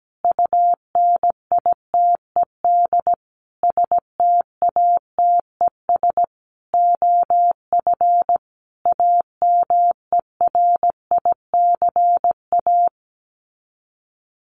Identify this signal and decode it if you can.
Morse code, heard as a keyed tone, about 17 wpm, 700 Hz. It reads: UNITED STATES OF AMERICA
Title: Sets of 5 Words 17wpm